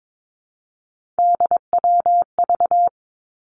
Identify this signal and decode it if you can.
Morse code, heard as a keyed tone, about 22 wpm, 700 Hz. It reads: DWV
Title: Three Challenging Letter-Number Combinations 22wpm